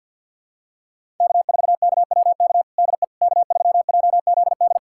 Morse code, heard as a keyed tone, about 46 wpm, 700 Hz. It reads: K4KWK DE K3JZD
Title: USA Callsign DE Callsign - 5 Characters - 50wpm